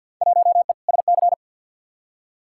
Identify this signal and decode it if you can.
Morse code, heard as a keyed tone, about 51 wpm, 700 Hz. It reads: 1E SC
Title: Field Day Exchanges - 50wpm